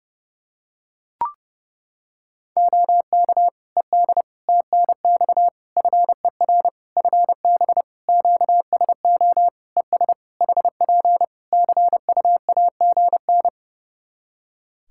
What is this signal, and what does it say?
Morse code, heard as a keyed tone, about 32 wpm, 700 Hz. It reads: OK ED TNX FER FB QSO ES HP CUAGN
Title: K1USN SST QSO Practice 30wpm